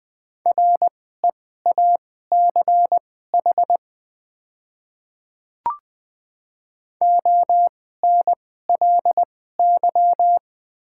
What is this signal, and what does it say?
Morse code, heard as a keyed tone, about 20 wpm, spaced as for 14 wpm, 700 Hz. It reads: REACH ONLY
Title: 2x Character Spacing - Top 500 Words - 20wpm